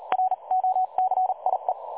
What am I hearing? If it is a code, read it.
Morse code, heard as a keyed tone, about 36 wpm, 740 Hz. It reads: ROCIE
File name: dlazaawansowanych / nagraniazpasma / websdr recording 3565.0kHz